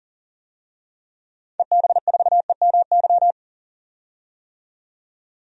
Morse code, heard as a keyed tone, about 40 wpm, 700 Hz. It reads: EB4EMY
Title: DXCC-100 Call Signs - 6 Characters - 40wpm